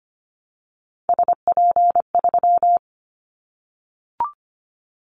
Morse code, heard as a keyed tone, about 25 wpm, 700 Hz. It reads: SP3
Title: Callsign Prefixes for Top 25 DXCC - 25wpm